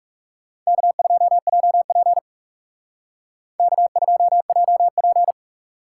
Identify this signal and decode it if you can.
Morse code, heard as a keyed tone, about 36 wpm, 700 Hz. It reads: K2JP K2JP
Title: Speed Racing - USA 1x2 Call Signs 30wpm